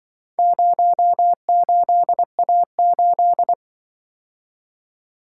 Morse code, heard as a keyed tone, about 24 wpm, 700 Hz. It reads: 08A8
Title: Speed-Racing - Four Letter-Number Combinations - 20wpm